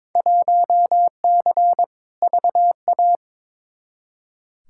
Morse code, heard as a keyed tone, about 22 wpm, 700 Hz. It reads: 1C VA
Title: Field Day Exchanges Encore - 22wpm